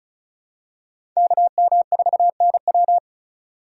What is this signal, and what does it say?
Morse code, heard as a keyed tone, about 36 wpm, 700 Hz. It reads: KM4NW
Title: USA Call Signs - 5 Characters - 4x Character Spacing - 35wpm